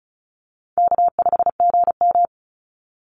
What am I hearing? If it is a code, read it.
Morse code, heard as a keyed tone, about 34 wpm, 700 Hz. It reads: K5GM
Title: Call Signs Repeated 3x - 35wpm